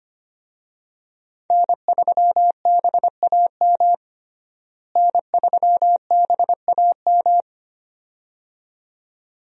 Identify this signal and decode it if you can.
Morse code, heard as a keyed tone, about 25 wpm, 700 Hz. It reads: N3BAM N3BAM
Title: USA Call Signs - 5 Characters - Repeated 2x - 25wpm